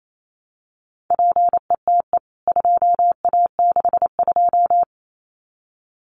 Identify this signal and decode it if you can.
Morse code, heard as a keyed tone, about 28 wpm, 700 Hz. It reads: PETE 2A62
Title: CWT Exchange 28wpm